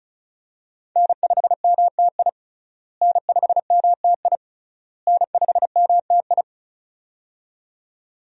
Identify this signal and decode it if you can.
Morse code, heard as a keyed tone, about 36 wpm, 700 Hz. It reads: N5MTI N5MTI N5MTI
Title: USA Call Signs - 5 Characters - Repeated 3x - 35wpm